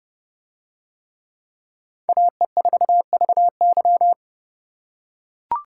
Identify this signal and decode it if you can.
Morse code, heard as a keyed tone, about 29 wpm, 700 Hz. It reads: AE4VY